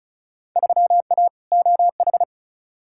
Morse code, heard as a keyed tone, about 36 wpm, 700 Hz. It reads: 3A OH
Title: Field Day Exchanges - 35wpm